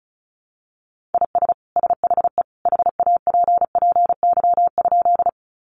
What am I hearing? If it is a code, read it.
Morse code, heard as a keyed tone, about 34 wpm, 700 Hz. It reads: IS SHE HAPPY?
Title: Sentences from Top 1000 Words 35wpm